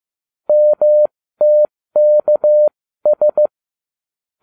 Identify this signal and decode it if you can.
Morse code, heard as a keyed tone, about 15 wpm, spaced as for 13 wpm, 600 Hz. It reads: MTKS